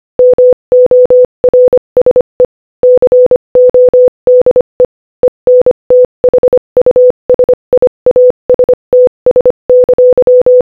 Morse code, heard as a keyed tone, about 25 wpm, 500 Hz. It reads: MORSE CODE ENTHUSIASTS!
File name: Morse-Code-Audio-Speed-25-12